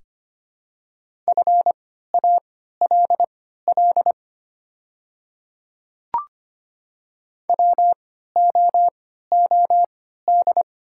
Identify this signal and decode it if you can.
Morse code, heard as a keyed tone, about 25 wpm, spaced as for 14 wpm, 700 Hz. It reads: FALL WOOD
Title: 3x Character Spacing - Top 500 Words - 25wpm